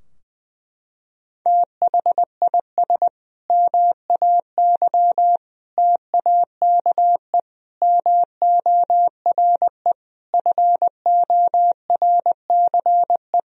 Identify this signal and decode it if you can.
Morse code, heard as a keyed tone, about 20 wpm, 700 Hz. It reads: THIS MAY TAKE MORE FORCE
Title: Sentences from Top 1000 Words - Review Part 3 - 20wpm